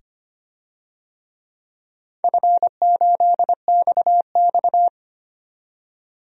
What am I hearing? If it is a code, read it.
Morse code, heard as a keyed tone, about 25 wpm, 700 Hz. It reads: F8XX
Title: DXCC-100 Call Signs - 4 Characters - 25wpm